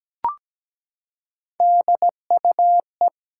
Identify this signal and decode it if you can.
Morse code, heard as a keyed tone, about 17 wpm, 700 Hz. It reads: DUE
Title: Three Letter Words - 17wpm